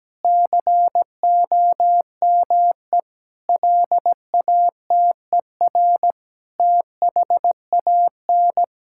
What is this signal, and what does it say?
Morse code, heard as a keyed tone, about 17 wpm, 700 Hz. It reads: COME LATER THAN
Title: Sentences from Top 1600 Words 17wpm